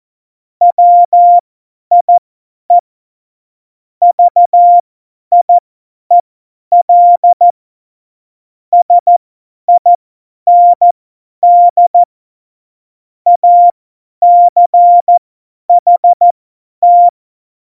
Morse code, heard as a keyed tone, about 14 wpm, spaced as for 10 wpm, 700 Hz. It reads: WIE VIEL SIND ACHT
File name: Morsekurs / Witze / Witz 23. April 2019, 50 F70